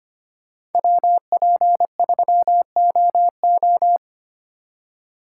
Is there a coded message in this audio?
Yes: Morse code from 0.8 to 4.0 s, heard as a keyed tone, about 32 wpm, 700 Hz. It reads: WP3OO